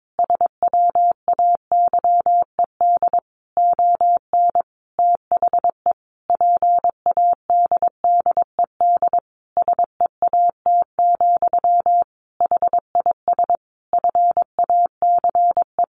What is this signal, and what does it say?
Morse code, heard as a keyed tone, about 22 wpm, 700 Hz. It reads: SWAYED ON THE PADDED SEAT, HIS FACE
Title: A Sound Of Thunder 22wpm